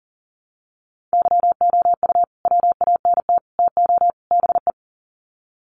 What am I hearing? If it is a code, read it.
Morse code, heard as a keyed tone, about 40 wpm, 700 Hz. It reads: YOU WANT TO BE
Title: Sets of 4 Words 40wpm